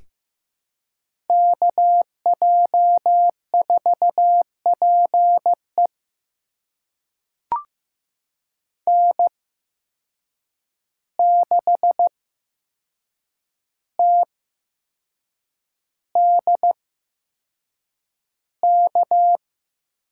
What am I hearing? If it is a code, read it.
Morse code, heard as a keyed tone, about 15 wpm, spaced as for 6 wpm, 700 Hz. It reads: KJ4PE N6TDK
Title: USA Call Signs - 5 Characters - 8x Character Spacing - 15wpm